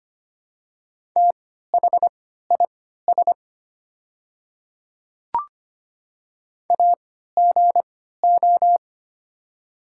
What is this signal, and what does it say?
Morse code, heard as a keyed tone, about 25 wpm, spaced as for 14 wpm, 700 Hz. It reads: THIS AGO
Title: 3x Character Spacing - Top 500 Words - 25wpm